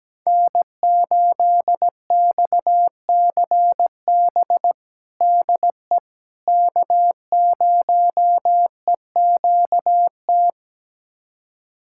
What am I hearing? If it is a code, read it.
Morse code, heard as a keyed tone, about 17 wpm, 700 Hz. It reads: N8XCB DE K0EQT
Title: USA Callsign DE Callsign - 5 Characters - 17wpm